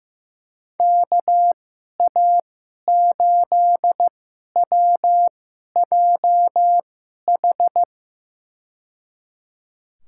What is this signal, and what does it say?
Morse code, heard as a keyed tone, about 15 wpm, spaced as for 11 wpm, 700 Hz. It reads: KA8WJH
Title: USA Call Signs - 6 Characters - 2x Character Spacing - 15wpm